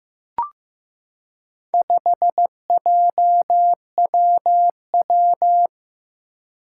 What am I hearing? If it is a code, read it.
Morse code, heard as a keyed tone, about 15 wpm, 700 Hz. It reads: 5JWW
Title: Four Challenging Letter-Number Combinations 15wpm